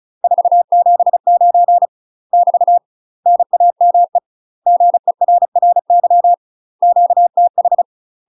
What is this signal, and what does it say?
Morse code, heard as a keyed tone, about 34 wpm, 700 Hz. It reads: 479 <BT> NAME GERRY QTH
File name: UK MASTER QSOs TRAINING / 35WPM / UK MASTER QSOs 0024